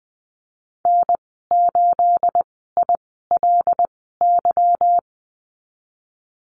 Morse code, heard as a keyed tone, about 20 wpm, spaced as for 15 wpm, 700 Hz. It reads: N8ILY
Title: 213 Lesson 34g - Number 8 - Call Signs 20wpm